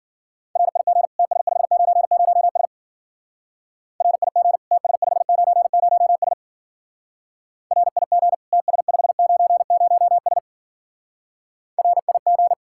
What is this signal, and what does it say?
Morse code, heard as a keyed tone, about 49 wpm, 700 Hz. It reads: RIG TS590S RIG TS590S RIG TS590S RIG
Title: Speed-Racing Common QSO Phrases - 40wpm